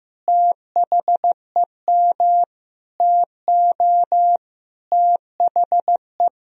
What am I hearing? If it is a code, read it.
Morse code, heard as a keyed tone, about 15 wpm, 700 Hz. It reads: THEM TO THE
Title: Sentences from Top 1000 Words - Review Part 3 - 15wpm